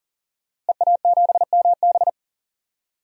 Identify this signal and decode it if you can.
Morse code, heard as a keyed tone, about 40 wpm, 700 Hz. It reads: EA7MB
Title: DXCC-100 Call Signs - 5 Characters - Repeated 2x - 40wpm